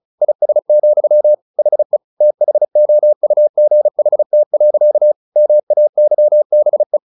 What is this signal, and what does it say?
Morse code, heard as a keyed tone, about 34 wpm, 600 Hz. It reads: IS, HE THOUGHT. MAYBE
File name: the old man and the sea 2 35wpm